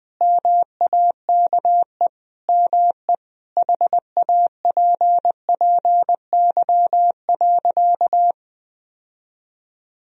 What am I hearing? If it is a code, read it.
Morse code, heard as a keyed tone, about 21 wpm, 700 Hz. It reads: MAKE ME HAPPY.
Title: Sentences from Top 1000 Words - Review Part 2 - 20wpm